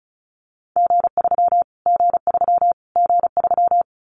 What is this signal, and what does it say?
Morse code, heard as a keyed tone, about 34 wpm, 700 Hz. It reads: G3 G3 G3
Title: Callsign Prefixes for Top 25 DXCC - Repeated 3x - 35wpm